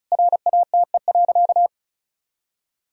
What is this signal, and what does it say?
Morse code, heard as a keyed tone, about 36 wpm, 700 Hz. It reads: RATE.
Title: Sentences from Top 1000 Words - Review Part 4 - 35wpm